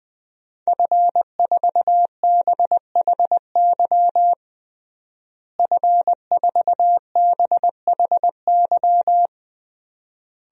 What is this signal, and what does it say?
Morse code, heard as a keyed tone, about 21 wpm, 700 Hz. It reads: F4BHY F4BHY
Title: DXCC-100 Call Signs - 5 Characters - Repeated 3x - 20wpm